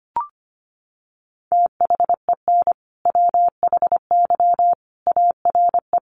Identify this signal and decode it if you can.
Morse code, heard as a keyed tone, about 25 wpm, 700 Hz. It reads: THEN WHY ARE
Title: Sentences from Top 200 Words 25wpm